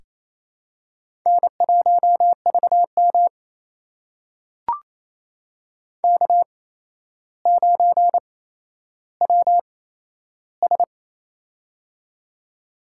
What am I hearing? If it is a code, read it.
Morse code, heard as a keyed tone, about 28 wpm, spaced as for 12 wpm, 700 Hz. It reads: N1VM K9WS
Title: USA Call Signs - 4 Characters - 8x Character Spacing - 28wpm